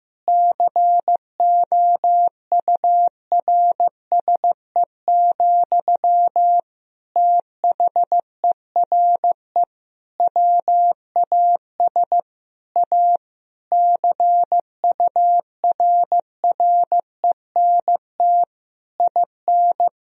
Morse code, heard as a keyed tone, about 15 wpm, 700 Hz. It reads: COURSE, THERE WAS A CURRENT IN
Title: Sentences from Top 1000 Words - Review Part 4 - 15wpm